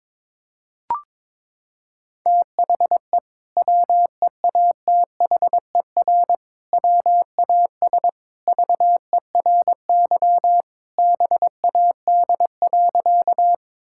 Morse code, heard as a keyed tone, about 21 wpm, 700 Hz. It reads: THE WEATHER WAS VERY BAD.
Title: Sentences from Top 1000 Words - Review Part 5 - 22wpm